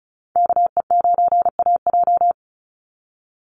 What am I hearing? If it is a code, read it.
Morse code, heard as a keyed tone, about 34 wpm, 700 Hz. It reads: KE9AJ
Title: NA SOTA Chaser Callsigns 35wpm